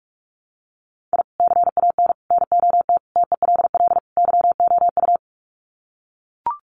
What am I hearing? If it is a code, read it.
Morse code, heard as a keyed tone, about 44 wpm, 700 Hz. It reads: I CAN NOT TELL YOU
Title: Sets of 5 Words 45wpm